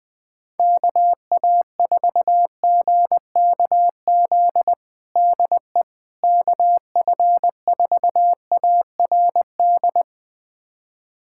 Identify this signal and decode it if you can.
Morse code, heard as a keyed tone, about 21 wpm, 700 Hz. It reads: KA4GKZ DE KF4ARD
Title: USA Callsign DE Callsign - 6 Characters - 20wpm